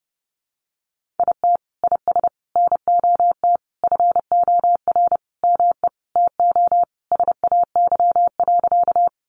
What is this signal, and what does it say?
Morse code, heard as a keyed tone, about 30 wpm, 700 Hz. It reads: IT IS NOT FOR ME TO SAY.
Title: Sentences from Top 600 Words 30wpm